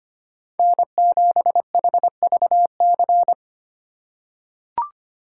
Morse code, heard as a keyed tone, about 25 wpm, 700 Hz. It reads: N7HVC
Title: USA Call Signs - 2x Character Spacing - 25wpm